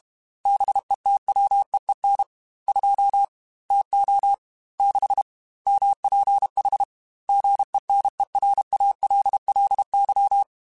Morse code, heard as a keyed tone, about 32 wpm, 800 Hz. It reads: BETWEEN 2 TO 6 MPH GENERALLY
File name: Weather Sets / Weather Sets-mp3s / Weather 4 10 @30